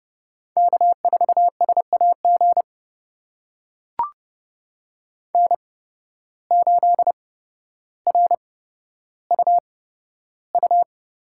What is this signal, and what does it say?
Morse code, heard as a keyed tone, about 30 wpm, spaced as for 13 wpm, 700 Hz. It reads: K4SAG N8RUU
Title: USA Call Signs - 5 Characters - 8x Character Spacing - 30wpm